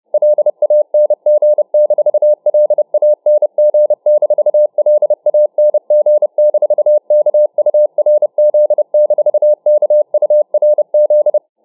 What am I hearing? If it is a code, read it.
Morse code, heard as a keyed tone, about 30 wpm, 600 Hz. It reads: LANG-LANG-LANG-KURZ-KURZ